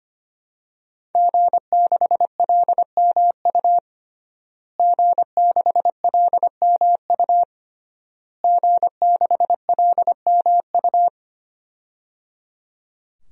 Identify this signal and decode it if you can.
Morse code, heard as a keyed tone, about 25 wpm, 700 Hz. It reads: G6LMU G6LMU G6LMU
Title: DXCC-100 Call Signs - 5 Characters - Repeated 3x - 25wpm